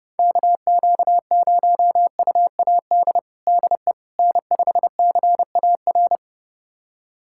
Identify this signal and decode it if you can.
Morse code, heard as a keyed tone, about 32 wpm, 700 Hz. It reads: KQ0UAD DE N5CAR
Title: USA Callsign DE Callsign 30wpm